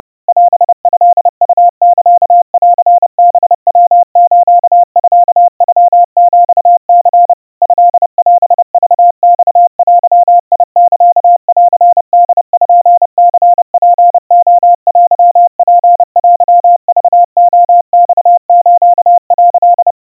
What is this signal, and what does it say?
Morse code (Japanese Wabun code), heard as a keyed tone, about 30 wpm, 700 Hz. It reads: オトウサンハヤスミノヒニ トオクマデサンポニツレテツテクレマス」
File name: kawa / Morse sample / kamyu 110 cpm 700 Hz Crispy